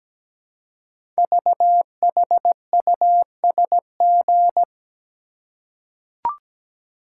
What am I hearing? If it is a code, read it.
Morse code, heard as a keyed tone, about 17 wpm, 700 Hz. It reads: VHUSG
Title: Code Groups - 5 Letters - 17wpm